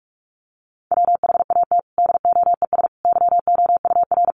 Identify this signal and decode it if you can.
Morse code, heard as a keyed tone, about 44 wpm, 700 Hz. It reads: WHAT DOES YOUR
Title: Sentences from Top 1500 Words 45wpm